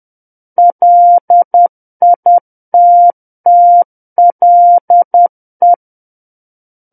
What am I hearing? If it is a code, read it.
Morse code, heard as a keyed tone, about 10 wpm, 700 Hz. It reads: LITTLE
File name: Book One / 01-Chapter 1 - 10 WPM